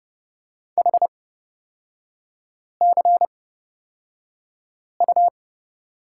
Morse code, heard as a keyed tone, about 30 wpm, spaced as for 5 wpm, 700 Hz. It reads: HCU
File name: Lesson 36e - BT - Three Characters 30wpm